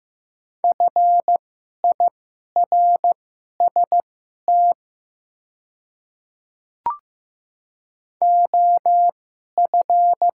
Morse code, heard as a keyed tone, about 15 wpm, spaced as for 11 wpm, 700 Hz. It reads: FIRST OF
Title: 2x Character Spacing - Top 200 Words - 15wpm